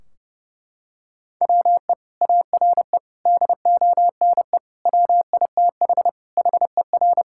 Morse code, heard as a keyed tone, about 30 wpm, 700 Hz. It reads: WE ARE DONE WITH HER
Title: Sentences from Top 1600 Words 30wpm